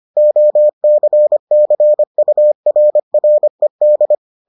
Morse code, heard as a keyed tone, about 25 wpm, 600 Hz. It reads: OCCURRED